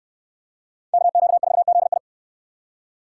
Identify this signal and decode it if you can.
Morse code, heard as a keyed tone, about 69 wpm, 700 Hz. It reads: KQ4ZI